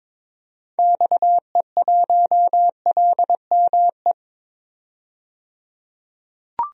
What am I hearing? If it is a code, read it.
Morse code, heard as a keyed tone, about 21 wpm, 700 Hz. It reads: XE1LME
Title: DXCC-100 Call Signs - 6 Characters - 22wpm